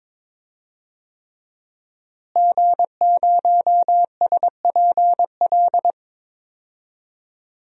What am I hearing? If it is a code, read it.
Morse code, heard as a keyed tone, about 22 wpm, 700 Hz. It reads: G0SPL
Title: DXCC-100 Call Signs - 5 Characters - Repeated 3x - 22wpm